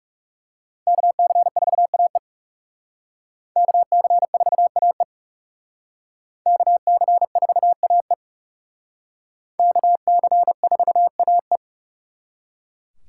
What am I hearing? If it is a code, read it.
Morse code, heard as a keyed tone, about 37 wpm, 700 Hz. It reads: KC4AE KC4AE KC4AE KC4AE
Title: Speed Racing - USA Call Signs 30wpm